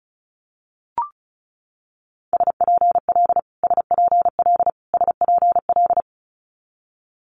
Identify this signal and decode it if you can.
Morse code, heard as a keyed tone, about 36 wpm, 700 Hz. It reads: SPL SPL SPL
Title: Multi-letter Phonemes Repeated 3x 35wpm